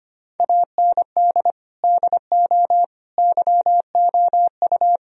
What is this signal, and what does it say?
Morse code, heard as a keyed tone, about 25 wpm, 700 Hz. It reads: AND DO YOU
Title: Sentences from Top 1000 Words - Review Part 3 - 25wpm